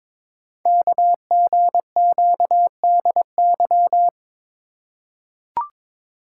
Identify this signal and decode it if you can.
Morse code, heard as a keyed tone, about 22 wpm, 700 Hz. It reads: KGQDY
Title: Code Groups - 5 Letters - 22wpm